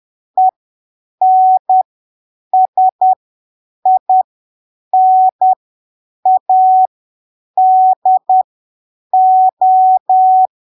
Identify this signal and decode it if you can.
Morse code, heard as a keyed tone, about 10 wpm, spaced as for 7 wpm, 750 Hz. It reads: ENSINADO
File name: arquivos / texto / texto 05 10ppm